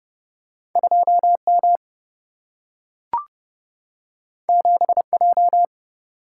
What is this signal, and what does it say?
Morse code, heard as a keyed tone, about 29 wpm, 700 Hz. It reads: 2M 7J
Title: Single-Letter Single-Number 30wpm